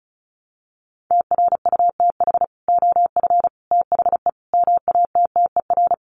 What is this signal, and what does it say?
Morse code, heard as a keyed tone, about 34 wpm, 700 Hz. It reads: TRUTH OF THE MATTER
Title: Sets of 5 Words Encore2 35wpm